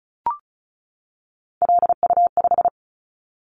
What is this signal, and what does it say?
Morse code, heard as a keyed tone, about 34 wpm, 700 Hz. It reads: LU5
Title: Callsign Prefixes for Top 100 DXCC - 35wpm